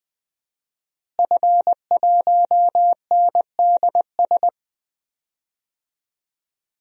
Morse code, heard as a keyed tone, about 20 wpm, 700 Hz. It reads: F1NDS
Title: DXCC-100 Call Signs - 5 Characters - Repeated 3x - 20wpm